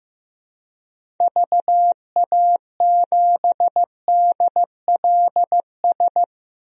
Morse code, heard as a keyed tone, about 15 wpm, 700 Hz. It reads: VA7DLS
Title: DXCC-100 Call Signs - 6 Characters - 15wpm